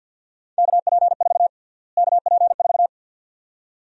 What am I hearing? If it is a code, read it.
Morse code, heard as a keyed tone, about 50 wpm, 700 Hz. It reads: KP4 KP4
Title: USA Call Sign Prefixes - Repeated 2x - 50wpm